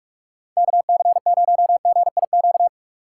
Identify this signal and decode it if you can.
Morse code, heard as a keyed tone, about 43 wpm, 700 Hz. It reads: KC0GIQ
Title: USA Call Signs - 6 Characters - 6x Character Spacing - 45wpm